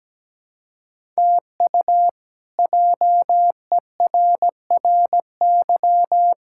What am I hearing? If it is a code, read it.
Morse code, heard as a keyed tone, about 17 wpm, 700 Hz. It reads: TU JERRY
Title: K1USN SST QSO Practice - 17wpm